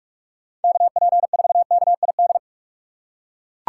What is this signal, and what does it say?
Morse code, heard as a keyed tone, about 41 wpm, 700 Hz. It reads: KP4KID